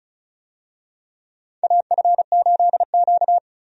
Morse code, heard as a keyed tone, about 36 wpm, 700 Hz. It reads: AF8Q
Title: USA Call Signs - 4 Characters - 35wpm